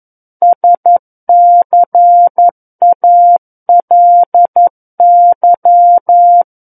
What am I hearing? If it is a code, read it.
Morse code, heard as a keyed tone, about 11 wpm, 700 Hz. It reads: SCALY
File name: 11 WPM